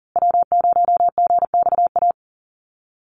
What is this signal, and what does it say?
Morse code, heard as a keyed tone, about 40 wpm, 700 Hz. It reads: W0GXA
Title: Call Signs Repeated 2x - 40wpm